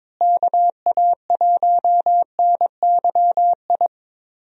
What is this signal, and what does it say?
Morse code, heard as a keyed tone, about 22 wpm, 700 Hz. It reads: KA1NYI
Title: USA Call Signs - 6 Characters - Repeated 2x - 22wpm